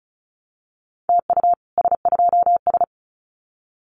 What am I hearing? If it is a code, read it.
Morse code, heard as a keyed tone, about 34 wpm, 700 Hz. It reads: TU S2S
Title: Common QSO Phrases Encore 35wpm